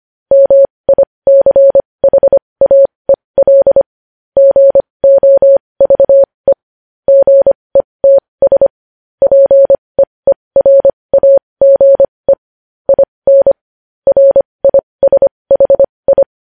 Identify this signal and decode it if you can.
Morse code, heard as a keyed tone, about 25 wpm, spaced as for 20 wpm, 550 Hz. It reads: MICHAEL GOVE GETS PEERAGE IN RISHI
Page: news headlines in morse code at 20 wpm